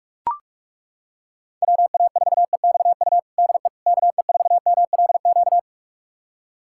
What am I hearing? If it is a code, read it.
Morse code, heard as a keyed tone, about 41 wpm, 700 Hz. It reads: WA4EXU DE KE4MLQ
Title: USA Callsign DE Callsign - 6 Characters - 45wpm